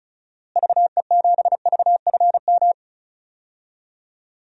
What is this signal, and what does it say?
Morse code, heard as a keyed tone, about 34 wpm, 700 Hz. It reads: VE7VFM
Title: DXCC-100 Call Signs - 6 Characters - Repeated 3x - 35wpm